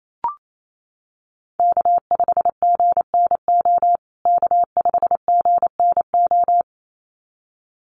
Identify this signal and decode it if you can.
Morse code, heard as a keyed tone, about 28 wpm, 700 Hz. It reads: K5GNO K5GNO
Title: USA Call Signs Repeated 2x 28wpm